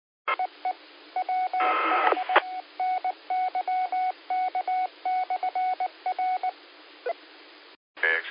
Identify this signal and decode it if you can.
Morse code, heard as a keyed tone, about 19 wpm, 740 Hz. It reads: E EWH NYK/R
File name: qsorec SimplexLogic 17-55-57